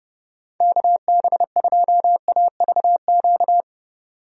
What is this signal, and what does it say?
Morse code, heard as a keyed tone, about 32 wpm, 700 Hz. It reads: KB2AVQ
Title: USA Call Signs - 6 Characters - 30wpm